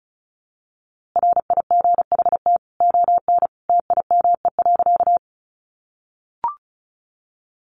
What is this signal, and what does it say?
Morse code, heard as a keyed tone, about 34 wpm, 700 Hz. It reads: RIGHT ON TIME.
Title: A Sound Of Thunder 35wpm